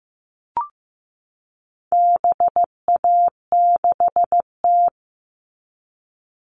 Morse code, heard as a keyed tone, about 15 wpm, 700 Hz. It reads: BA6T
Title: All Cut Numbers - 4-Digit 15wpm